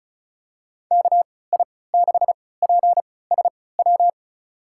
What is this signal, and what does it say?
Morse code, heard as a keyed tone, about 34 wpm, spaced as for 20 wpm, 700 Hz. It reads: KI6PSW
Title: USA Call Signs - 3x Character Spacing - 35wpm